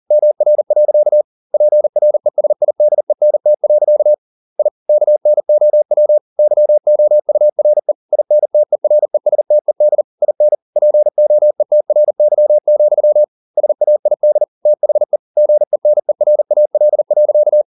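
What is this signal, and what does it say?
Morse code, heard as a keyed tone, about 40 wpm, 600 Hz. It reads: MR. PRESIDENT. I KNOW YOURE INTERESTED IN POETRY, SAID THE GENERAL.
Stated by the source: Morse code at 40 wpm, practice text, CW med Gnister.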